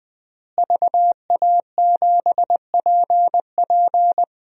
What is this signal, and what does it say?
Morse code, heard as a keyed tone, about 20 wpm, 700 Hz. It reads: VA7PP
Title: DXCC-100 Call Signs - 5 Characters - Repeated 3x - 20wpm